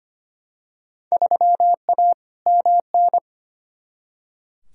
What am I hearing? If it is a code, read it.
Morse code, heard as a keyed tone, about 25 wpm, 700 Hz. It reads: 3A MN
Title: Field Day Exchanges - 25wpm